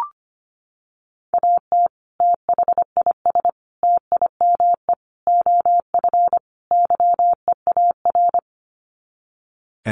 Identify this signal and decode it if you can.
Morse code, heard as a keyed tone, about 25 wpm, 700 Hz. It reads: AT THIS TIME OF YEAR
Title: Sets of 5 Words Encore2 25wpm